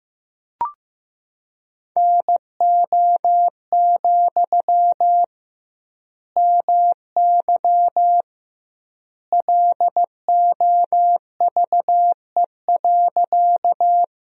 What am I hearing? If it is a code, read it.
Morse code, heard as a keyed tone, about 16 wpm, 700 Hz. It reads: NO, MY LOVE.
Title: 2x Word-Spacing - Sentences from Top 400 Words - 15wpm